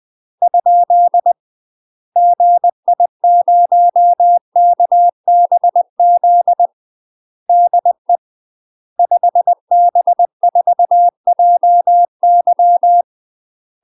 Morse code, heard as a keyed tone, about 20 wpm, 700 Hz. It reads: ? GI0KBZ DE 5B4JY
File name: UK MASTER QSOs TRAINING / 20WPM / UK MASTER QSOs 0037